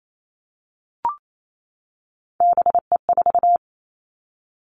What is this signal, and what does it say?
Morse code, heard as a keyed tone, about 28 wpm, 700 Hz. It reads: BE4
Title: Three Challenging Letter-Number Combinations 28wpm